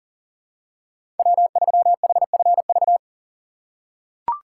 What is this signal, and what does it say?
Morse code, heard as a keyed tone, about 40 wpm, 700 Hz. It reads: W3HFV
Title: USA Call Signs - 5 Characters - 3x Character Spacing - 40wpm